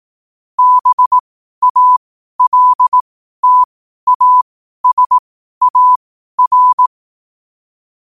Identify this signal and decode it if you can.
Morse code, heard as a keyed tone, about 18 wpm, spaced as for 12 wpm, 1000 Hz. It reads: BALTASAR
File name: qotdfiles / 12wpm